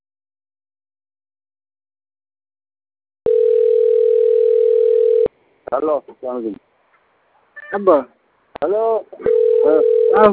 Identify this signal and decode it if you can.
Ringback tone (North America), 440 + 480 Hz, with speech.
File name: autotestagent / 11-20-12.136..callcentertransfer. .15271a0c-2a2a-4f62-9c65- d604d